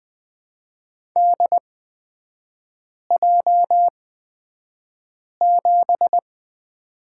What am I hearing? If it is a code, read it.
Morse code, heard as a keyed tone, about 20 wpm, spaced as for 5 wpm, 700 Hz. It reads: DJ7
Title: 224 Lesson 36e - BT - Three Characters 20wpm